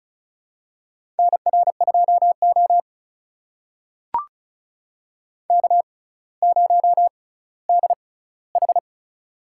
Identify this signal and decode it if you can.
Morse code, heard as a keyed tone, about 36 wpm, spaced as for 18 wpm, 700 Hz. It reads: NR2O K0DH
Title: USA Call Signs - 4 Characters - 6x Character Spacing - 35wpm